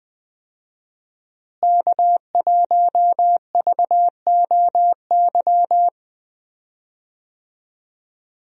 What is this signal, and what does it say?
Morse code, heard as a keyed tone, about 20 wpm, 700 Hz. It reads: K1VOY
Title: USA Call Signs - 5 Characters - Repeated 2x - 20wpm